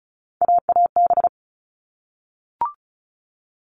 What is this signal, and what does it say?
Morse code, heard as a keyed tone, about 34 wpm, 700 Hz. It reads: AAB
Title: All Cut Numbers - 3-Digit 35wpm